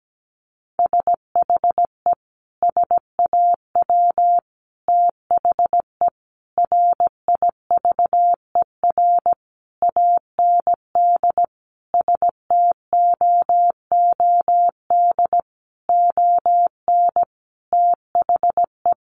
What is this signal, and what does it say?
Morse code, heard as a keyed tone, about 17 wpm, 700 Hz. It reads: SHE SAW THE RIVER AND STOOD ON THE
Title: Sentences from Top 1200 Words 17wpm